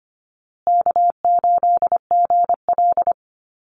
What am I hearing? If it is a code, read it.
Morse code, heard as a keyed tone, about 25 wpm, 700 Hz. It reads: K8GL
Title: Easy Call Signs Repeated 3x - 25wpm